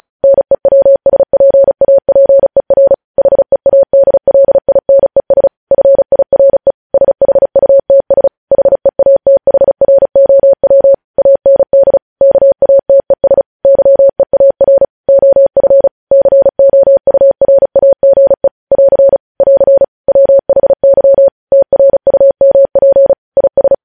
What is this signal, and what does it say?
Morse code, heard as a keyed tone, about 36 wpm, 550 Hz. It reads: NEWSPAPER HEADLINES FIRE SHUTS HEATHROW AND KATES YEAR OF COURAGE <AR> <AR> WHY TRUMP IS
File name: podcast / News headlines-1x-35x35